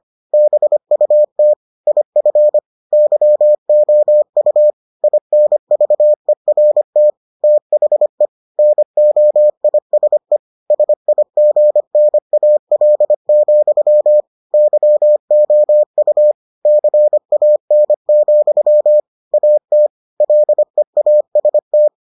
Morse code, heard as a keyed tone, about 25 wpm, 600 Hz. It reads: BUT IF YOU INVERT THE NOISE SIGNAL, YOU CAN, AT LEAST